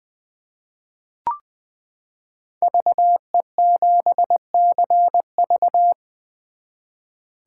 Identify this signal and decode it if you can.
Morse code, heard as a keyed tone, about 21 wpm, 700 Hz. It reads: VE7CV
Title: 2023 SOTA Chaser Call Signs - 20wpm